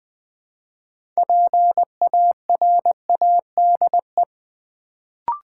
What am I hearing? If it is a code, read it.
Morse code, heard as a keyed tone, about 20 wpm, 700 Hz. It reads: PARADE
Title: Build A Word - 2 Letters At A Time - 6-Letter Words - 20wpm